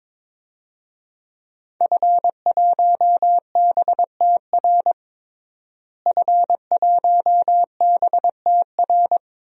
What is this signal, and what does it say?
Morse code, heard as a keyed tone, about 22 wpm, 700 Hz. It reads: F1BTR F1BTR
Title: DXCC-100 Call Signs - 5 Characters - Repeated 3x - 22wpm